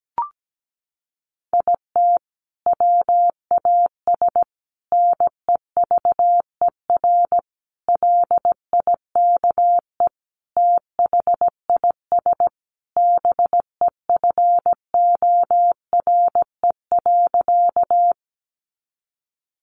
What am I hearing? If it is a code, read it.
Morse code, heard as a keyed tone, about 17 wpm, 700 Hz. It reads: IT WAS NEVER LIKE THIS BEFORE.
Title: A Sound Of Thunder 17wpm